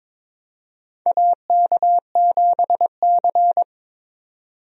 Morse code, heard as a keyed tone, about 21 wpm, 700 Hz. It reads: AK7C
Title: USA Call Signs - 4 Characters - 22wpm